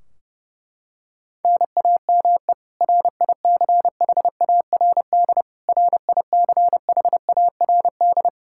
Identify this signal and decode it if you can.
Morse code, heard as a keyed tone, about 30 wpm, 700 Hz. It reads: NAME RICHARD RICHARD
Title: Practice QSO - Exchange 1 - Beginning - 30wpm